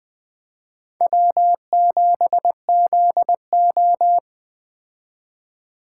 Morse code, heard as a keyed tone, about 20 wpm, 700 Hz. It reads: W7ZO
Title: Speed Racing - USA 1x2 Call Signs 17wpm